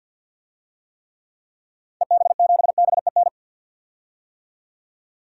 Morse code, heard as a keyed tone, about 53 wpm, 700 Hz. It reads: EB7BEN
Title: DXCC-100 Call Signs - 6 Characters - 50wpm